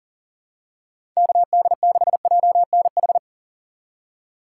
Morse code, heard as a keyed tone, about 40 wpm, 700 Hz. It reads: KD6JNH
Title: USA Call Signs - 3x Character Spacing - 40wpm